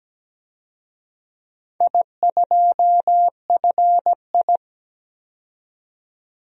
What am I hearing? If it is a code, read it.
Morse code, heard as a keyed tone, about 17 wpm, 700 Hz. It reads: I2FI
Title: DXCC-100 Call Signs - 4 Characters - 17wpm